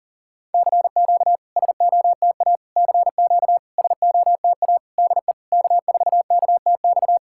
Morse code, heard as a keyed tone, about 42 wpm, 690 Hz. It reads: CQ SOTA CQ SOTA DE K4KTX
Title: SOTA QSO Practice 40wpm